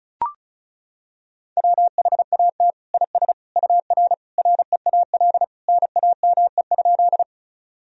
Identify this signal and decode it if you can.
Morse code, heard as a keyed tone, about 36 wpm, 700 Hz. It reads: WHAT IS UR REAL NAME?
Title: Sentences from Top 300 Words and Abbreviations 35wpm